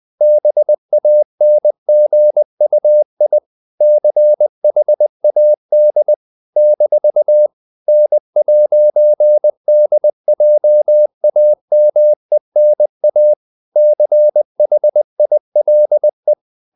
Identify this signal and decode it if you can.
Morse code, heard as a keyed tone, about 20 wpm, 600 Hz. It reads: BANGUI CHAD - N'DJAMENA CHILE
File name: CW / Practice / NationalCapitals20